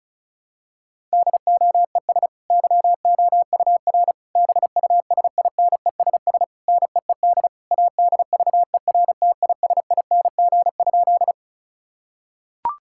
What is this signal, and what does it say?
Morse code, heard as a keyed tone, about 36 wpm, 700 Hz. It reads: DOES YOUR BUSINESS NEED ADVERTISING?
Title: Sentences from Top 1000 Words - Review Part 6 - 35wpm